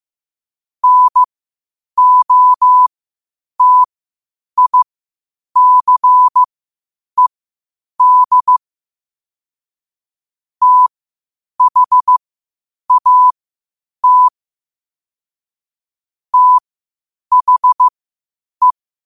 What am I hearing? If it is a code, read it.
Morse code, heard as a keyed tone, about 15 wpm, spaced as for 8 wpm, 1000 Hz. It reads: NOTICED THAT THE
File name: qotdfiles / 8wpm